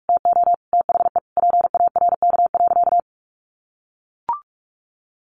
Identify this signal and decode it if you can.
Morse code, heard as a keyed tone, about 44 wpm, 700 Hz. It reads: TO THE PARK.
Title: Sentences from Top 1400 Words 45wpm